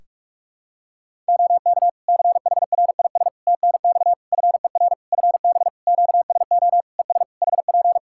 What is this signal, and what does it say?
Morse code, heard as a keyed tone, about 43 wpm, 700 Hz. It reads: OK CHRIS TNX FER FB QSO ES HP
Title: Practice QSO - Exchange 4 - Ending - 45wpm